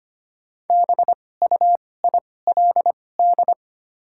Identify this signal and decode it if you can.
Morse code, heard as a keyed tone, about 25 wpm, spaced as for 18 wpm, 700 Hz. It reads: BUILD
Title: 2x Character Spacing - Top 300 Words - 25wpm